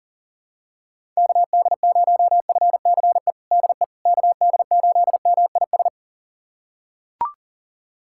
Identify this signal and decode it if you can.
Morse code, heard as a keyed tone, about 40 wpm, 700 Hz. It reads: KD0FCE DE KD8MIS
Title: USA Callsign DE Callsign 40wpm